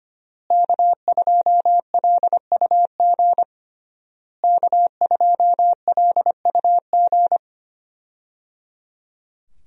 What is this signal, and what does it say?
Morse code, heard as a keyed tone, about 25 wpm, 700 Hz. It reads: K2LUG K2LUG
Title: USA Call Signs - 5 Characters - Repeated 3x - 25wpm